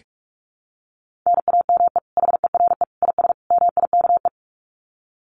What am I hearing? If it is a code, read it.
Morse code, heard as a keyed tone, about 44 wpm, 700 Hz. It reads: NAME HERE IS MIKE
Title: Common QSO Phrases 45wpm